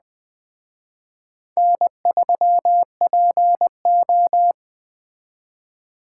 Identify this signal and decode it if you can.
Morse code, heard as a keyed tone, about 21 wpm, 700 Hz. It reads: N3PO
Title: Speed Racing - USA 1x2 Call Signs 17wpm